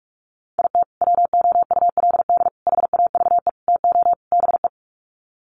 Morse code, heard as a keyed tone, about 44 wpm, 700 Hz. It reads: IT WOULD HAVE TO BE
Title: Sets of 5 Words Encore 45wpm